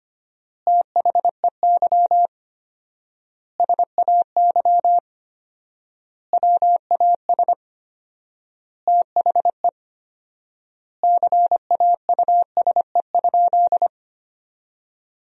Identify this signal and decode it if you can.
Morse code, heard as a keyed tone, about 25 wpm, 700 Hz. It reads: THEY SAY WAS THE CAUSE?
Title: 4x Word-Spacing - Sentences from Top 200 Words - 25wpm